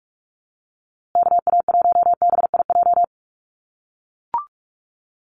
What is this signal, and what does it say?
Morse code, heard as a keyed tone, about 44 wpm, 700 Hz. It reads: KA1BIJ
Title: USA Call Signs Repeated 2x 45wpm